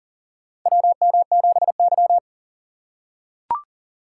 Morse code, heard as a keyed tone, about 37 wpm, 700 Hz. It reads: WM7Y
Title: USA Call Signs - 4 Characters - 3x Character Spacing - 40wpm